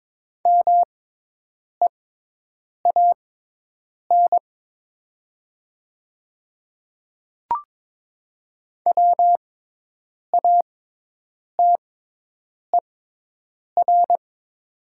Morse code, heard as a keyed tone, about 22 wpm, spaced as for 8 wpm, 700 Hz. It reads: MEAN WATER
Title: 6x Character Spacing - Top 200 Words - 22wpm